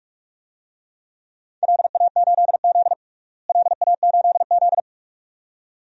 Morse code, heard as a keyed tone, about 46 wpm, 700 Hz. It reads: LA8Z LA8Z
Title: DXCC-100 Call Signs - 4 Characters - Repeated 2x - 45wpm